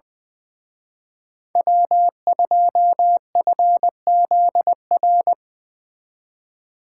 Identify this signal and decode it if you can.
Morse code, heard as a keyed tone, about 24 wpm, 700 Hz. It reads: W2FZR
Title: Speed Racing - USA 1x3 Call Signs 20wpm